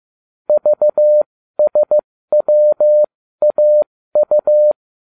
Morse code, heard as a keyed tone, about 15 wpm, spaced as for 13 wpm, 600 Hz. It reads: VSWAU
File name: cw / ubacw20 2